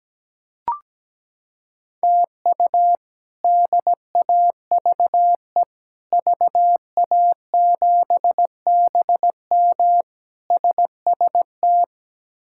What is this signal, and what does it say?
Morse code, heard as a keyed tone, about 17 wpm, 700 Hz. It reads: TU DAVE VA7BM SST
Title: K1USN SST QSO Practice - 17wpm